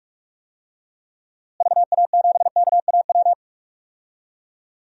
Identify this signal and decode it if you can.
Morse code, heard as a keyed tone, about 46 wpm, 700 Hz. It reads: VA7KAW
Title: DXCC-100 Call Signs - 6 Characters - Repeated 2x - 45wpm